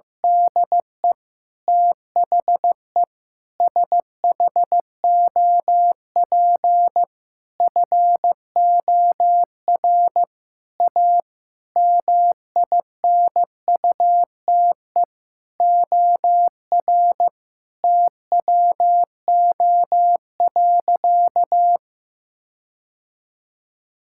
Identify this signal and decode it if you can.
Morse code, heard as a keyed tone, about 15 wpm, 700 Hz. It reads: DE THE SHOP FOR A MINUTE OR TWO.
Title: Sentences from Top 1600 Words 15wpm